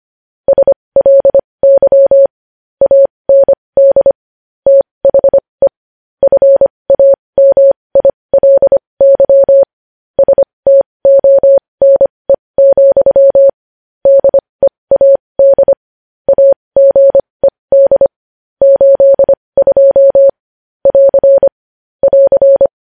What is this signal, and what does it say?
Morse code, heard as a keyed tone, about 25 wpm, spaced as for 20 wpm, 550 Hz. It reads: SLY AND THE FAMILY STONE, DEAD AGED 82 <AR> <AR>
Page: news-headlines-in-morse-code-at-20-wpm